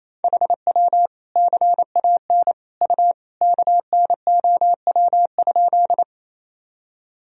Morse code, heard as a keyed tone, about 28 wpm, 700 Hz. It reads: HW CAN U KNOW?
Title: Sentences from Top 100 Words and Abbreviations 28wpm